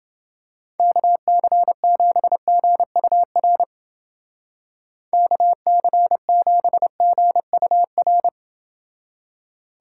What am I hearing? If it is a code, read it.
Morse code, heard as a keyed tone, about 28 wpm, 700 Hz. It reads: KC7GUR KC7GUR